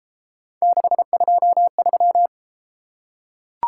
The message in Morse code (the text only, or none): 623